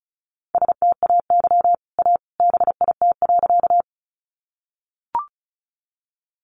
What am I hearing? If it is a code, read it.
Morse code, heard as a keyed tone, about 34 wpm, 700 Hz. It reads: STAY A BIT.
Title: Sentences from Top 900 Words 35wpm